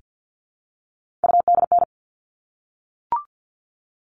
Morse code, heard as a keyed tone, about 50 wpm, 700 Hz. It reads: UDN
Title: All Cut Numbers - 3-Digit 50wpm